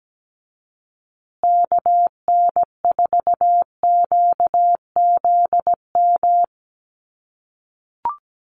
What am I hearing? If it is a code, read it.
Morse code, heard as a keyed tone, about 17 wpm, 700 Hz. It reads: KN4QZM
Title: USA Call Signs Repeated 3x 17wpm